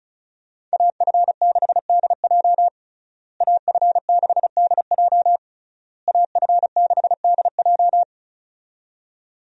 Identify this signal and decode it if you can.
Morse code, heard as a keyed tone, about 36 wpm, 700 Hz. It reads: AF6DJ AF6DJ AF6DJ
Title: USA Call Signs - 5 Characters - Repeated 3x - 35wpm